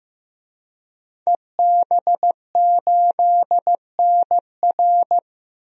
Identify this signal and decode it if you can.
Morse code, heard as a keyed tone, about 15 wpm, 700 Hz. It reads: EB8NR
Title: DXCC-100 Call Signs - 5 Characters - Repeated 2x - 15wpm